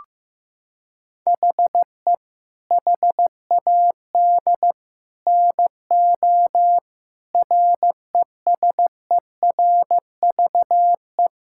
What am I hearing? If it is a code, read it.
Morse code, heard as a keyed tone, about 15 wpm, 700 Hz. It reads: HE HAD NO RESERVE
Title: Sentences from Top 1000 Words - Review Part 6 - 15wpm